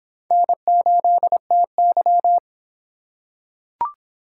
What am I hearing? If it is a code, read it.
Morse code, heard as a keyed tone, about 24 wpm, 700 Hz. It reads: N8TY